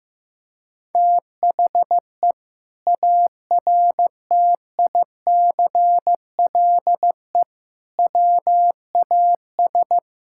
Morse code, heard as a keyed tone, about 15 wpm, 700 Hz. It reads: THE ARTICLE WAS
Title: Sentences from Top 1000 Words - Review Part 5 - 15wpm